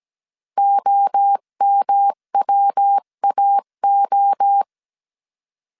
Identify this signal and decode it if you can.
Morse code, heard as a keyed tone, about 17 wpm, 800 Hz. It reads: OMWAO
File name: CW / DET / 1 3 1500C S A 16wpm 2141